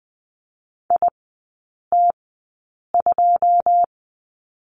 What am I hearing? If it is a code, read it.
Morse code, heard as a keyed tone, about 20 wpm, spaced as for 8 wpm, 700 Hz. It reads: IT2
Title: 093 Lesson 16e - Letter c - Three Characters 20wpm